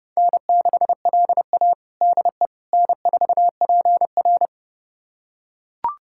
Morse code, heard as a keyed tone, about 30 wpm, 700 Hz. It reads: N6LA DE N4PR
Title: USA Callsign DE Callsign - 4 Characters - 30wpm